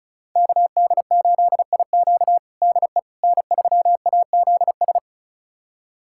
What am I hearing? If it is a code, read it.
Morse code, heard as a keyed tone, about 36 wpm, 700 Hz. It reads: KD8IQ DE N3AZS
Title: USA Callsign DE Callsign - 5 Characters - 35wpm